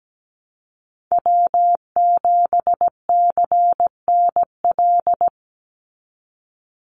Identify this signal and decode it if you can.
Morse code, heard as a keyed tone, about 17 wpm, 700 Hz. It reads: W7CNL
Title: NA SOTA Chaser Callsigns 17wpm